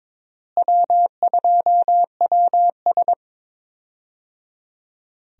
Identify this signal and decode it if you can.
Morse code, heard as a keyed tone, about 22 wpm, 700 Hz. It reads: W2WS
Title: USA Call Signs - 4 Characters - Repeated 3x - 22wpm